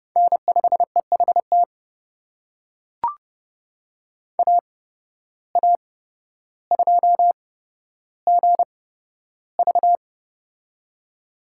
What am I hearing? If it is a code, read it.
Morse code, heard as a keyed tone, about 29 wpm, spaced as for 13 wpm, 700 Hz. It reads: N5EHT AA2GV
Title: USA Call Signs - 5 Characters - 8x Character Spacing - 30wpm